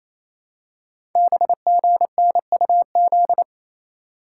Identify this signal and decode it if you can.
Morse code, heard as a keyed tone, about 28 wpm, 700 Hz. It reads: BGNUZ
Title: Code Groups - 5 Letters - 28wpm